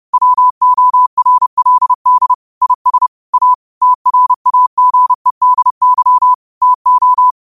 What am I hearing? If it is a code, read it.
Morse code, heard as a keyed tone, about 31 wpm, 1000 Hz. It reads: WORLD IS A TRAGEDY TO
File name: 30wpm